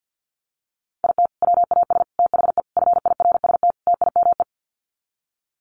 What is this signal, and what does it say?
Morse code, heard as a keyed tone, about 50 wpm, 700 Hz. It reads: IT WAS THE FIRST TIME
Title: Sets of 5 Words 50wpm